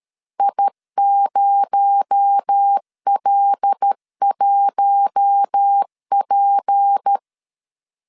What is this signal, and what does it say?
Morse code, heard as a keyed tone, about 13 wpm, 800 Hz. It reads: I0L1P
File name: CW / DET / 1 9 1500C S A 12wpm 3800